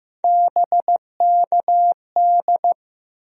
Morse code, heard as a keyed tone, about 15 wpm, 700 Hz. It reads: BKD
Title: USA Call Sign Postfixes - 3 Letters 15wpm